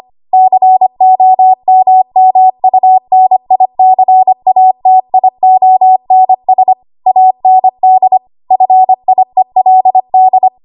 Morse code, heard as a keyed tone, about 25 wpm, 750 Hz. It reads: COMMUNICATIONS AND FIELD
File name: Archive / 25 WPM / 25WPM